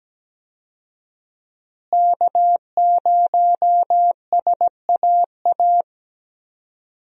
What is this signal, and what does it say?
Morse code, heard as a keyed tone, about 17 wpm, 700 Hz. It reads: K0SAA
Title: USA Call Signs - 5 Characters - Repeated 3x - 17wpm